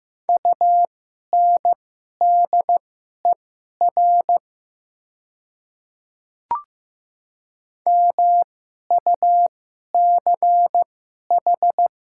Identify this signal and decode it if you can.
Morse code, heard as a keyed tone, about 15 wpm, spaced as for 11 wpm, 700 Hz. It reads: UNDER MUCH
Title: 2x Character Spacing - Top 300 Words - 15wpm